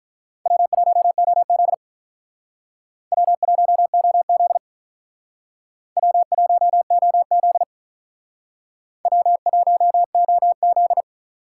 Morse code, heard as a keyed tone, about 42 wpm, 700 Hz. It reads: W1OZ W1OZ W1OZ W1OZ
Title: Speed Racing - USA 1x2 Call Signs 35wpm